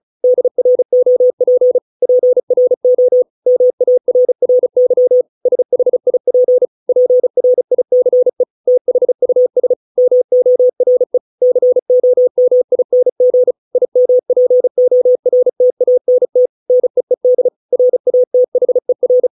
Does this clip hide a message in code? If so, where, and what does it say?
Yes: Morse code from 0.2 to 19.3 s, heard as a keyed tone, about 35 wpm, 500 Hz. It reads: DROP PRO MARRY SHIP PRICE THUS MORE COMING IMPORTANT NEED RATHER